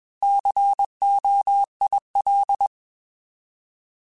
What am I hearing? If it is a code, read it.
Morse code, heard as a keyed tone, about 21 wpm, 800 Hz. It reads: COIL
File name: Alphabetical / Alphabetical mp3s / Cs @20